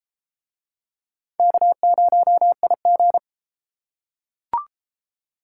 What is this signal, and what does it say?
Morse code, heard as a keyed tone, about 32 wpm, 700 Hz. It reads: K0IG